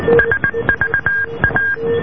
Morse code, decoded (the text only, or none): SVA